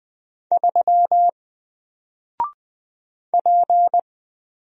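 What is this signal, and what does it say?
Morse code, heard as a keyed tone, about 20 wpm, 700 Hz. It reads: 3 P
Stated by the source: Single letter and number Morse drill at 20 wpm, from N4LCD.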